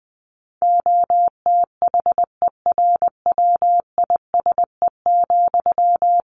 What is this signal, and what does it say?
Morse code, heard as a keyed tone, about 20 wpm, 700 Hz. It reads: OTHERWISE,
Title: Sentences from Top 1400 Words 20wpm